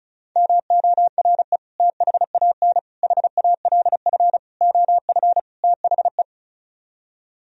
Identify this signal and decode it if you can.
Morse code, heard as a keyed tone, about 45 wpm, 700 Hz. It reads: MORE THAN HALF OF THE